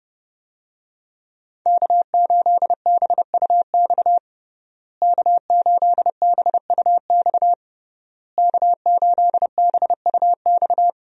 Morse code, heard as a keyed tone, about 30 wpm, 700 Hz. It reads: K8BUX K8BUX K8BUX
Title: USA Call Signs - 5 Characters - Repeated 3x - 30wpm